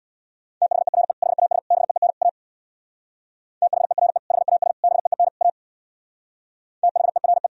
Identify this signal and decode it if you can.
Morse code, heard as a keyed tone, about 62 wpm, 690 Hz. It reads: THERE HAS BEEN A THERE HAS BEEN A THERE
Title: Speed-Racing - Sets of 4 Words 50wpm